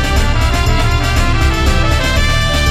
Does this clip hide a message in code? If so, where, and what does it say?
No.